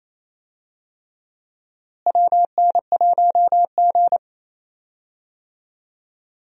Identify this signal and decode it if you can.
Morse code, heard as a keyed tone, about 28 wpm, 700 Hz. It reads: WN1G
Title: USA Call Signs - 4 Characters - Repeated 3x - 28wpm